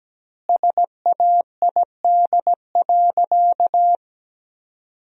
Morse code, heard as a keyed tone, about 17 wpm, 700 Hz. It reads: SAID.
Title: 2x Word-Spacing - Sentences from Top 100 Words - 17wpm